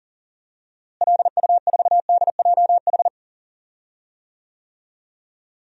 Morse code, heard as a keyed tone, about 40 wpm, 700 Hz. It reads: LU4DJH
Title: DXCC-100 Call Signs - 6 Characters - Repeated 2x - 40wpm